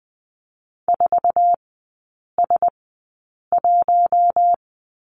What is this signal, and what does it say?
Morse code, heard as a keyed tone, about 20 wpm, spaced as for 8 wpm, 700 Hz. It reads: 4S1
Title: 047 Lesson 9e - Number 4 - Three Characters 20wpm